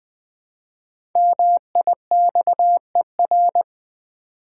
Morse code, heard as a keyed tone, about 20 wpm, 700 Hz. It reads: MIXER
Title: Easy Words with Hints - 20wpm